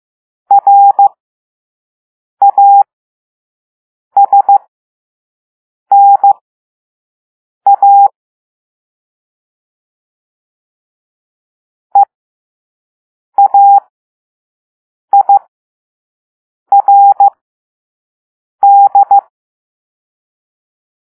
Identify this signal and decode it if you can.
Morse code, heard as a keyed tone, about 15 wpm, spaced as for 5 wpm, 800 Hz. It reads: RASNA EAIRD
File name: Morse / Level3-All-B